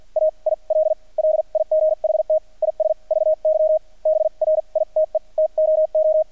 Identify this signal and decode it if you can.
Morse code, heard as a keyed tone, about 48 wpm, spaced as for 39 wpm, 620 Hz. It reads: AIL LIGHT IS VY BRITE TOO
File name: archived / CRSnet